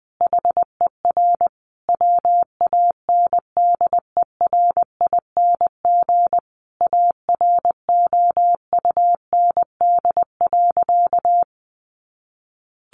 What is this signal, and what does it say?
Morse code, heard as a keyed tone, about 19 wpm, 700 Hz. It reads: HER WANDERING AROUND.
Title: Additional Easy Phrases 20wpm